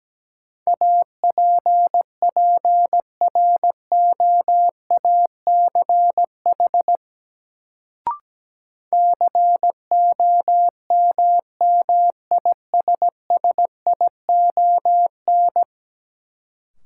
Morse code, heard as a keyed tone, about 17 wpm, 700 Hz. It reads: APPROACH COMMISSION
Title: Sentences from Top 1400 Words - New Word Drill Alt - 17wpm